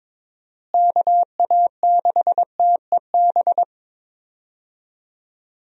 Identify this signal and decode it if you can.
Morse code, heard as a keyed tone, about 22 wpm, 700 Hz. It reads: KA6TEB
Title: USA Call Signs - 6 Characters - Repeated 3x - 22wpm